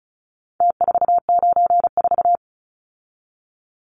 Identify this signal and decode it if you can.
Morse code, heard as a keyed tone, about 34 wpm, 700 Hz. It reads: T494
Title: Numbers - Contest Style - 4-Digit 35wpm